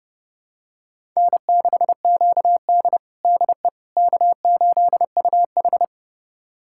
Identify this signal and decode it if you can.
Morse code, heard as a keyed tone, about 30 wpm, 700 Hz. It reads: N6QD DE K8UH
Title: USA Callsign DE Callsign - 4 Characters - 30wpm